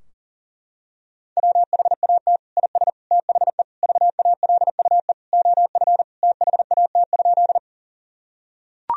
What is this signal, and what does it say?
Morse code, heard as a keyed tone, about 40 wpm, 700 Hz. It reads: WHAT IS THE VALUE OF THAT?
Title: Sentences from Top 1000 Words - Review Part 3 - 40wpm